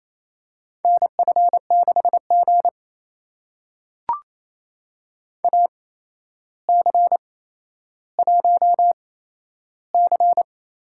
Morse code, heard as a keyed tone, about 28 wpm, spaced as for 12 wpm, 700 Hz. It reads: NF6G AC1C
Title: USA Call Signs - 4 Characters - 8x Character Spacing - 28wpm